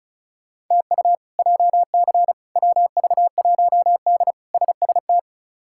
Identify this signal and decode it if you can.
Morse code, heard as a keyed tone, about 36 wpm, 700 Hz. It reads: TU JC WV1D SST
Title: K1USN SST QSO Practice - 35wpm